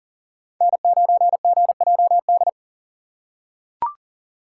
Morse code, heard as a keyed tone, about 40 wpm, 700 Hz. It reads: N9GJD
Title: USA Call Signs - 5 Characters - 2x Character Spacing - 40wpm